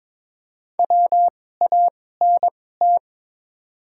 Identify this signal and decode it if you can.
Morse code, heard as a keyed tone, about 22 wpm, spaced as for 16 wpm, 700 Hz. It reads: WANT
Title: 2x Character Spacing - Top 300 Words - 22wpm